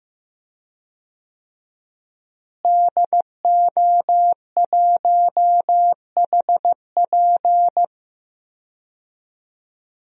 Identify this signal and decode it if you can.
Morse code, heard as a keyed tone, about 15 wpm, 700 Hz. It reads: DO1HP
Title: DXCC-100 Call Signs - 5 Characters - Repeated 2x - 15wpm